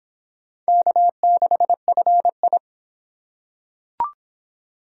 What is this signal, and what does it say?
Morse code, heard as a keyed tone, about 23 wpm, 700 Hz. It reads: K6FI